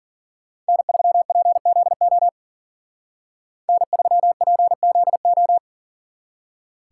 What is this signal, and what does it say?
Morse code, heard as a keyed tone, about 52 wpm, 700 Hz. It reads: N3PZO N3PZO